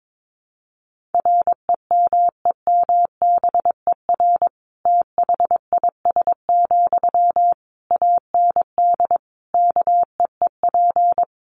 Morse code, heard as a keyed tone, about 22 wpm, 700 Hz. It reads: REMEMBER THIS, AND KEEP
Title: Sentences from Top 1500 Words 22wpm